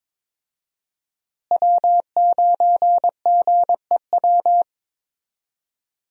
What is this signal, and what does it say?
Morse code, heard as a keyed tone, about 21 wpm, 700 Hz. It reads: W9GEW
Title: USA Call Signs - 5 Characters - Repeated 2x - 22wpm